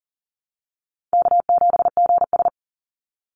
Morse code, heard as a keyed tone, about 40 wpm, 700 Hz. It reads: K7GS
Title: Easy Call Signs Repeated 3x - 40wpm